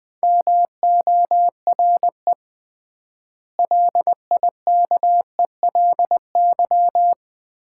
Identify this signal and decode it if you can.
Morse code, heard as a keyed tone, about 20 wpm, 700 Hz. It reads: MORE LIKELY
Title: 3x Word-Spacing - Sets of 4 Words - 20wpm